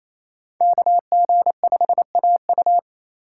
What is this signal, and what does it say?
Morse code, heard as a keyed tone, about 28 wpm, 700 Hz. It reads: KG5AU
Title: USA Call Signs - 5 Characters - 4x Character Spacing - 28wpm